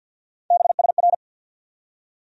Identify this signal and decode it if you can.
Morse code, heard as a keyed tone, about 46 wpm, 700 Hz. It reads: BSR
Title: USA Call Sign Postfixes - 3 Letters 50wpm